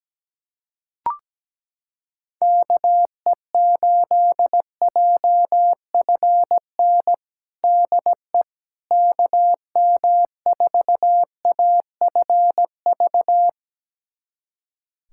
Morse code, heard as a keyed tone, about 17 wpm, 700 Hz. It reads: KE8JFN DE KM4AFV
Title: USA Callsign DE Callsign - 6 Characters - 17wpm